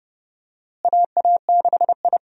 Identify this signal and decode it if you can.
Morse code, heard as a keyed tone, about 30 wpm, 700 Hz. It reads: AA6I
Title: USA Call Signs - 4 Characters - Repeated 3x - 30wpm